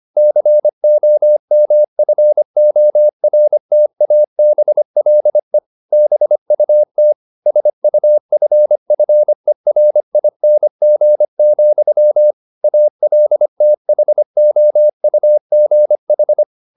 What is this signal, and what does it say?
Morse code, heard as a keyed tone, about 25 wpm, 600 Hz. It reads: COMFORTABLE BUT SUFFERING, ALTHOUGH
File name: the old man and the sea part 1 25wpm